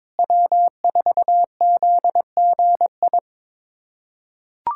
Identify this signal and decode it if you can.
Morse code, heard as a keyed tone, about 22 wpm, 700 Hz. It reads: W4ZGI
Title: USA Callsign DE Callsign 22wpm